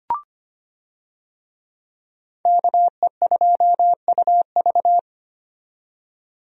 Morse code, heard as a keyed tone, about 25 wpm, 700 Hz. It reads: KE2UV
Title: USA Call Signs - 5 Characters - 25wpm